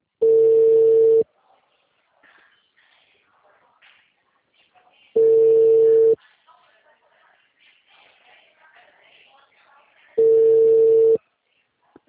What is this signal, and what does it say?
Ringback tone (China, 450 Hz).
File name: out- -2011- - - .36345